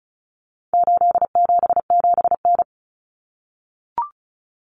Morse code, heard as a keyed tone, about 34 wpm, 700 Hz. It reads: 877N
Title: Common Cut Numbers - 4-Digit 35wpm